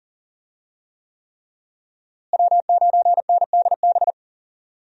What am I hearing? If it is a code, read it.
Morse code, heard as a keyed tone, about 37 wpm, 700 Hz. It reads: W9NDB
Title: USA Call Signs - 5 Characters - Repeated 3x - 40wpm